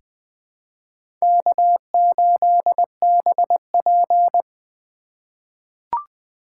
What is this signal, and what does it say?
Morse code, heard as a keyed tone, about 20 wpm, 700 Hz. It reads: K8BP
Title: USA Call Signs - 2x Character Spacing - 20wpm